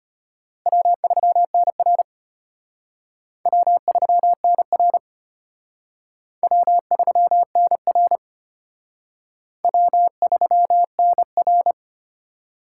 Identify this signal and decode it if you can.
Morse code, heard as a keyed tone, about 31 wpm, 700 Hz. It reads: W3NR W3NR W3NR W3NR